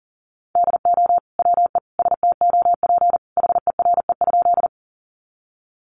Morse code, heard as a keyed tone, about 40 wpm, 700 Hz. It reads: DO WE STOP HERE?
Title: Sentences from Top 300 Words 40wpm